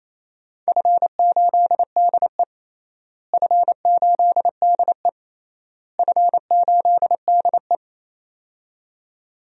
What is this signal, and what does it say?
Morse code, heard as a keyed tone, about 28 wpm, 700 Hz. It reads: F8DE F8DE F8DE
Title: DXCC-100 Call Signs - 4 Characters - Repeated 3x - 28wpm